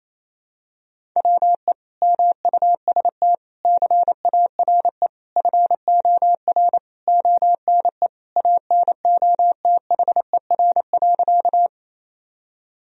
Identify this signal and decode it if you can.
Morse code, heard as a keyed tone, about 28 wpm, 700 Hz. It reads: WE MUST CARE FOR ONE ANOTHER.
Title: Sentences from Top 1000 Words - Review Part 1 - 28wpm